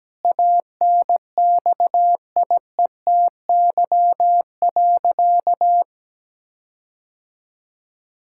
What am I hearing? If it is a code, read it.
Morse code, heard as a keyed tone, about 17 wpm, 700 Hz. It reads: ANXIETY.
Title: Sentences from Top 1600 Words 17wpm